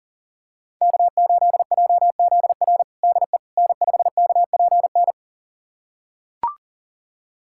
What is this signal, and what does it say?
Morse code, heard as a keyed tone, about 40 wpm, 700 Hz. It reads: K8JZR DE N5KPN
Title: USA Callsign DE Callsign - 5 Characters - 40wpm